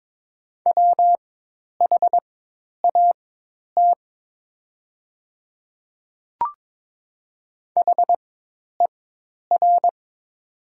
Morse code, heard as a keyed tone, about 22 wpm, spaced as for 10 wpm, 700 Hz. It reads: WHAT HER
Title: 4x Character Spacing - Top 100 Words - 22wpm